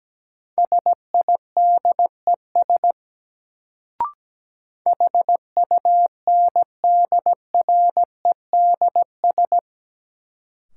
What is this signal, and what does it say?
Morse code, heard as a keyed tone, about 17 wpm, 700 Hz. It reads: SIDES HUNDREDS
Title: Sets of 4 Words - New Word Drill - 17wpm